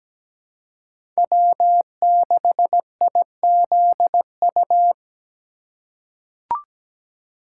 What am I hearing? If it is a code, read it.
Morse code, heard as a keyed tone, about 17 wpm, 700 Hz. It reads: W6IZU
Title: USA Call Signs - 5 Characters - 2x Character Spacing - 17wpm